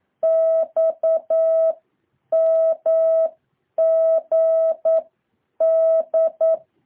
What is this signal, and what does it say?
Morse code, heard as a keyed tone, about 9 wpm, 650 Hz. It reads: X M G D
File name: jh8cbh / 2023cw